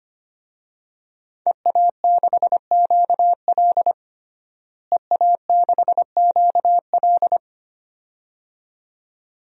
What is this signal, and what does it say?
Morse code, heard as a keyed tone, about 25 wpm, 700 Hz. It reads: EA6QL EA6QL
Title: DXCC-100 Call Signs - 5 Characters - Repeated 2x - 25wpm